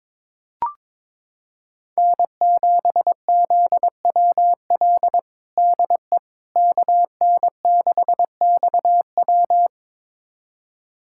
Morse code, heard as a keyed tone, about 21 wpm, 700 Hz. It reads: N7ZWL DE KN6XW
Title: USA Callsign DE Callsign - 5 Characters - 22wpm